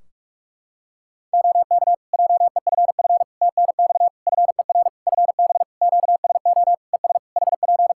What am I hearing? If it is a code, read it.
Morse code, heard as a keyed tone, about 46 wpm, 700 Hz. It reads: OK JEFF TNX FER FB QSO ES HP
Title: K1USN SST QSO Practice 45wpm